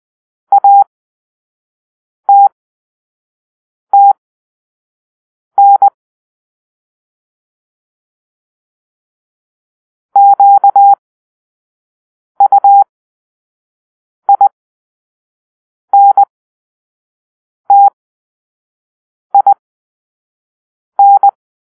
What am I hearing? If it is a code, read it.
Morse code, heard as a keyed tone, about 20 wpm, spaced as for 5 wpm, 800 Hz. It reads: ATTN QUINTIN
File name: arrl / Grp-6 Drill-4